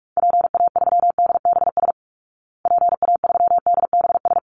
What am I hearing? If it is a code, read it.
Morse code, heard as a keyed tone, about 44 wpm, 700 Hz. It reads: PA3DBS PA3DBS
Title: Call Signs Repeated 3x - 45wpm